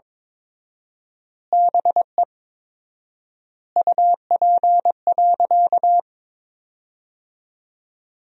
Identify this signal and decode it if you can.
Morse code, heard as a keyed tone, about 23 wpm, 700 Hz. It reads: BE UP.
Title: 4x Word-Spacing - Sentences from Top 300 Words - 22wpm